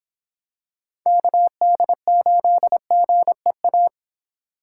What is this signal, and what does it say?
Morse code, heard as a keyed tone, about 26 wpm, 700 Hz. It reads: KD8GEA